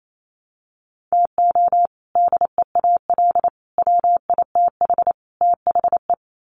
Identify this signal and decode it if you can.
Morse code, heard as a keyed tone, about 28 wpm, 700 Hz. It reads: TO DEAL WITH THE
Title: Sets of 4 Words 28wpm